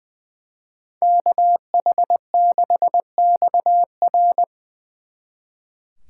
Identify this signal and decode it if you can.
Morse code, heard as a keyed tone, about 23 wpm, 700 Hz. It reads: KH6XR
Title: Speed Racing - USA 2x2 Call Signs 20wpm